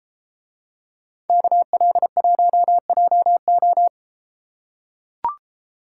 Morse code, heard as a keyed tone, about 31 wpm, 700 Hz. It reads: KL1JO